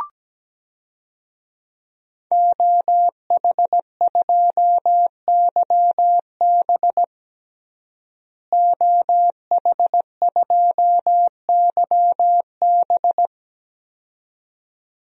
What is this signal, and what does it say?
Morse code, heard as a keyed tone, about 17 wpm, 700 Hz. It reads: OH2YB OH2YB
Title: DXCC-100 Call Signs - 5 Characters - Repeated 2x - 17wpm